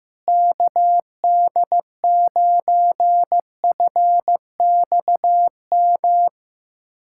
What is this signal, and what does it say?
Morse code, heard as a keyed tone, about 16 wpm, 700 Hz. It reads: KD9FXM
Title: USA Callsign DE Callsign 15wpm